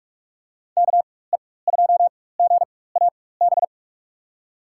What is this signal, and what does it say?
Morse code, heard as a keyed tone, about 43 wpm, spaced as for 21 wpm, 700 Hz. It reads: KE2GAB
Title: USA Call Signs - 6 Characters - 4x Character Spacing - 45wpm